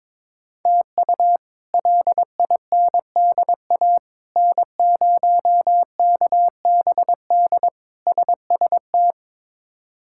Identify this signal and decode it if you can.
Morse code, heard as a keyed tone, about 22 wpm, 700 Hz. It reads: TU LINDA N0KBD SST
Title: K1USN SST QSO Practice - 22wpm